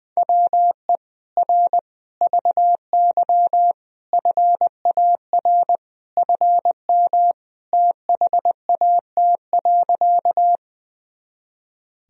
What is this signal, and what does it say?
Morse code, heard as a keyed tone, about 20 wpm, 700 Hz. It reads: WE R VY FAR FM THAT.
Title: Sentences from Top 300 Words and Abbreviations 20wpm